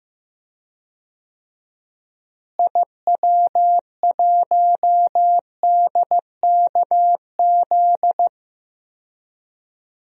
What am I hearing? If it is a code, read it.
Morse code, heard as a keyed tone, about 15 wpm, 700 Hz. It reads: IW1DKZ
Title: DXCC-100 Call Signs - 6 Characters - Repeated 3x - 15wpm